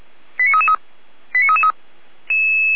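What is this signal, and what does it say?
Five-tone selective call (ZVEI), heard as quick runs of five tones, tones of about 70 ms. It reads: 89333 89333
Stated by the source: Five-tone ZVEI selective call received on SDR.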